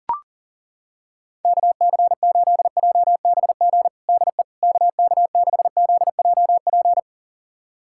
Morse code, heard as a keyed tone, about 40 wpm, 700 Hz. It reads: KC8JBG DE KK6ZJP
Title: USA Callsign DE Callsign - 6 Characters - 40wpm